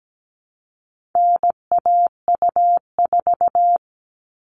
Morse code, heard as a keyed tone, about 17 wpm, 700 Hz. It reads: NAU4
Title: All Cut Numbers - 4-Digit 17wpm